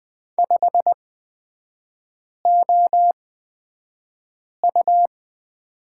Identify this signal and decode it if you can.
Morse code, heard as a keyed tone, about 20 wpm, spaced as for 5 wpm, 700 Hz. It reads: 5OU
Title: 229 Lesson 37e - Letter x - Three Characters 20wpm